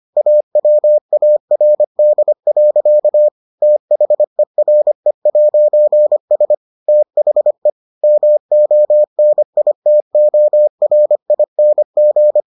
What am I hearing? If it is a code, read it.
Morse code, heard as a keyed tone, about 25 wpm, 600 Hz. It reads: AWARD. THERE'S THE MONITORING